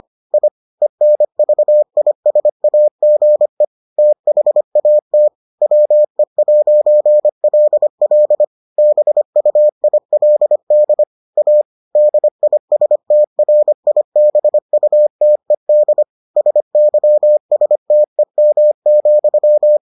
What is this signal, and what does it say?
Morse code, heard as a keyed tone, about 25 wpm, 600 Hz. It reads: I ENVISAGE THAT WE'LL BUILD A DISTRIBUTED SYSTEM,